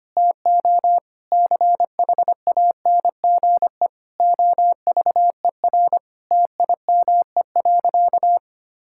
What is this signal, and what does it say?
Morse code, heard as a keyed tone, about 25 wpm, 700 Hz. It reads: TO CHANGE OVER TIME.
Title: Sentences from Top 1000 Words - Review Part 6 - 25wpm